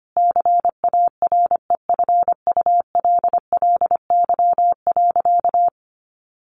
Morse code, heard as a keyed tone, about 25 wpm, 700 Hz. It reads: CAREFULLY.
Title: Sentences from Top 1300 Words 25wpm